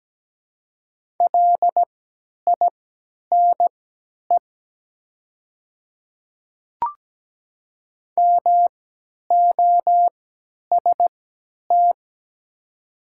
Morse code, heard as a keyed tone, about 17 wpm, spaced as for 10 wpm, 700 Hz. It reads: LINE MOST
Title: 3x Character Spacing - Top 200 Words - 17wpm